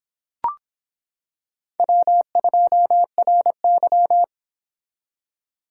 Morse code, heard as a keyed tone, about 26 wpm, 700 Hz. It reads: W2RY